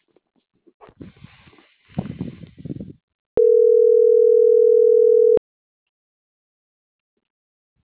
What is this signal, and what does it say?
Ringback tone (North America), 440 + 480 Hz.